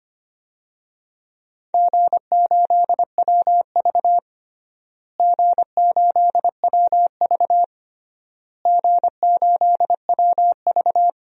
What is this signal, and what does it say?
Morse code, heard as a keyed tone, about 25 wpm, 700 Hz. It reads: G8WV G8WV G8WV
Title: DXCC-100 Call Signs - 4 Characters - Repeated 3x - 25wpm